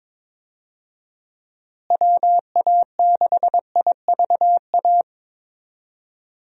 Morse code, heard as a keyed tone, about 22 wpm, 700 Hz. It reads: WA6IVA
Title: USA Call Signs - 6 Characters - Repeated 3x - 22wpm